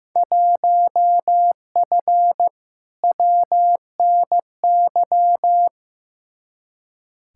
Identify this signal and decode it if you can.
Morse code, heard as a keyed tone, about 15 wpm, 700 Hz. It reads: 1F WNY
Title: Field Day Exchanges - 15wpm